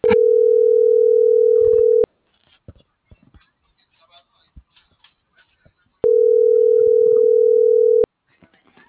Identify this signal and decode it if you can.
Ringback tone (North America), 440 + 480 Hz.
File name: CENTRAL TERRESTRE / out ramon.lamorena.2024